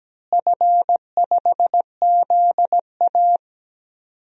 Morse code, heard as a keyed tone, about 17 wpm, 700 Hz. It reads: F5ZA
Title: DXCC-100 Call Signs - 4 Characters - Repeated 3x - 17wpm